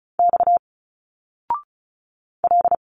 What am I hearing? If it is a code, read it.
Morse code, heard as a keyed tone, about 34 wpm, 700 Hz. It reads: X L
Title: Single Letters 35wpm